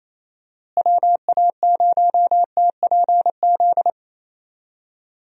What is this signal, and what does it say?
Morse code, heard as a keyed tone, about 28 wpm, 700 Hz. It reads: WA0TPZ
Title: USA Call Signs - 6 Characters - Repeated 2x - 28wpm